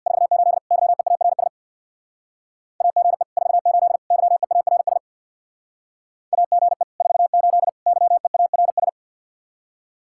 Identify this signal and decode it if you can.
Morse code, heard as a keyed tone, about 61 wpm, 700 Hz. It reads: 48 YEARS AGE 48 YEARS AGE 48 YEARS
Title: Speed-Racing Common QSO Phrases - 50wpm